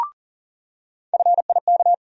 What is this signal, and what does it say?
Morse code, heard as a keyed tone, about 40 wpm, 700 Hz. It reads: FIK
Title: USA Call Sign Postfixes - 3 Letters 40wpm